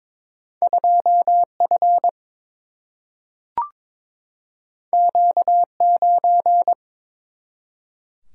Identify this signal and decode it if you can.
Morse code, heard as a keyed tone, about 22 wpm, 700 Hz. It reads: 2 F Q 9
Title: Single-Letter Single-Number 22wpm